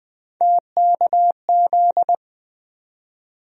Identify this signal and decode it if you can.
Morse code, heard as a keyed tone, about 20 wpm, 700 Hz. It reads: TKZ
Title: USA Call Sign Postfixes - 3 Letters 20wpm